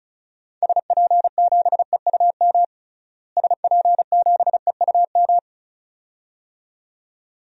Morse code, heard as a keyed tone, about 34 wpm, 700 Hz. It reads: SP7EUM SP7EUM
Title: DXCC-100 Call Signs - 6 Characters - Repeated 2x - 35wpm